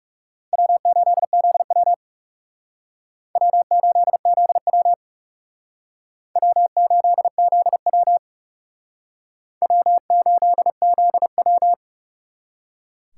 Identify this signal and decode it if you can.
Morse code, heard as a keyed tone, about 38 wpm, 700 Hz. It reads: W8ZW W8ZW W8ZW W8ZW